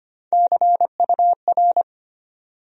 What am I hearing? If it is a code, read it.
Morse code, heard as a keyed tone, about 25 wpm, 700 Hz. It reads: CUR
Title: USA Call Sign Postfixes - 3 Letters 25wpm